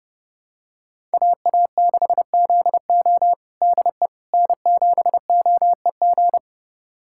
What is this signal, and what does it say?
Morse code, heard as a keyed tone, about 30 wpm, 700 Hz. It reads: AA6ZO DE N7OEG
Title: USA Callsign DE Callsign - 5 Characters - 30wpm